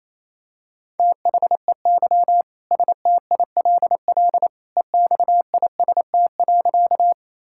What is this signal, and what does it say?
Morse code, heard as a keyed tone, about 28 wpm, 700 Hz. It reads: THEY STILL EXIST.
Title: Sentences from Top 1000 Words - Review Part 5 - 28wpm